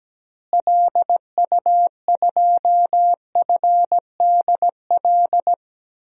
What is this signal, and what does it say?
Morse code, heard as a keyed tone, about 17 wpm, 700 Hz. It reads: LU2FDL
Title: DXCC-100 Call Signs - 6 Characters - Repeated 3x - 17wpm